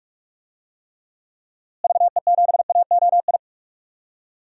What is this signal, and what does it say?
Morse code, heard as a keyed tone, about 46 wpm, 700 Hz. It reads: VE7AOI
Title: DXCC-100 Call Signs - 6 Characters - Repeated 3x - 45wpm